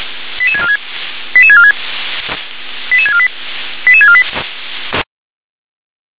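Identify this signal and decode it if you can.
Five-tone selective call (ZVEI), heard as quick runs of five tones, tones of about 70 ms. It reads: 88646 88646 88648 88648
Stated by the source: Five-tone ZVEI selective call received on an SDR.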